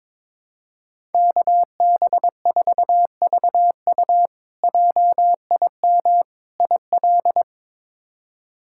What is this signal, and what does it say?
Morse code, heard as a keyed tone, about 21 wpm, 700 Hz. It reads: KB4VU JIM IL
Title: K1USN SST QSO Practice - 22wpm